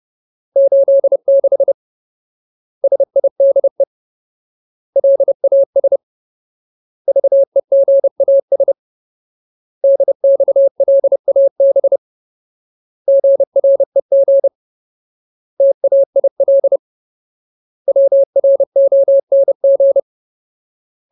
Morse code, heard as a keyed tone, about 30 wpm, 550 Hz. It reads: 86 SIDE LAS VEGAS DXLAB GREG TAIL WRONG
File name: Ultimate Word List / 30WPM Ultimate